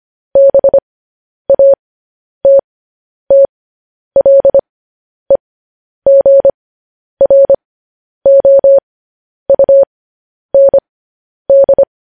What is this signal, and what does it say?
Morse code, heard as a keyed tone, about 25 wpm, spaced as for 10 wpm, 550 Hz. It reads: BATTLEGROUND